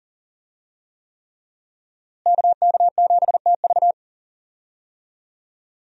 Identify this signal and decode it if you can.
Morse code, heard as a keyed tone, about 40 wpm, 700 Hz. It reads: KK7TV
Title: USA Call Signs - 5 Characters - Repeated 3x - 40wpm